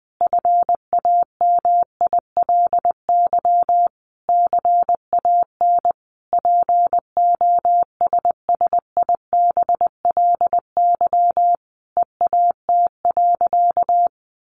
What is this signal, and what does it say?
Morse code, heard as a keyed tone, about 20 wpm, 700 Hz. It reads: FAMILY CAN POSSIBLY EAT.
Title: Sentences from Top 1300 Words 20wpm